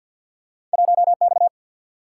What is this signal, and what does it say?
Morse code, heard as a keyed tone, about 53 wpm, 700 Hz. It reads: 1X
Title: Single-Letter Single-Number 50wpm